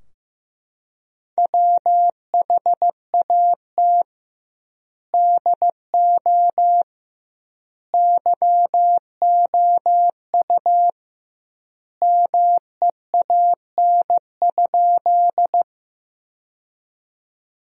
Morse code, heard as a keyed tone, about 15 wpm, 700 Hz. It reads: WHAT DO YOU MEAN?
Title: 2x Word-Spacing - Sentences from Top 200 Words - 15wpm